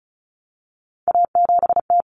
Morse code, heard as a keyed tone, about 34 wpm, 700 Hz. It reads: A7T
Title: Common Cut Numbers - 3-Digit 35wpm